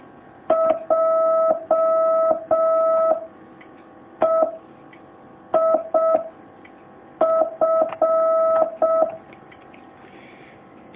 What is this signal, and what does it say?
Morse code, heard as a keyed tone, about 6 wpm, 650 Hz. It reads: J E I F
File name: jh8cbh / 2019cw